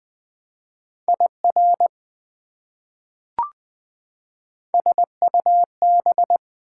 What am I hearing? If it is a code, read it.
Morse code, heard as a keyed tone, about 20 wpm, 700 Hz. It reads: IR SUB
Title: Prefixes and Suffixes 20wpm